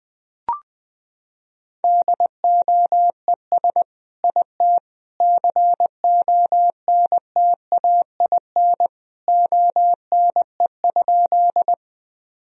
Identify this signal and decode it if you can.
Morse code, heard as a keyed tone, about 20 wpm, 700 Hz. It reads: DOES IT CONTAIN ONE?
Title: Sentences from Top 1000 Words - Review Part 2 - 20wpm